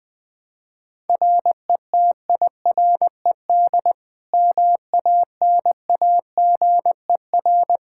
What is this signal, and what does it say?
Morse code, heard as a keyed tone, about 20 wpm, 700 Hz. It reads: RETIRED MANAGER
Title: Practice QSO - Exchange 3 - More Info - 20wpm